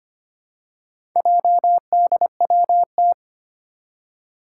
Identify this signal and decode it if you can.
Morse code, heard as a keyed tone, about 25 wpm, 700 Hz. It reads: JDWT
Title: Four Challenging Letter-Number Combinations 25wpm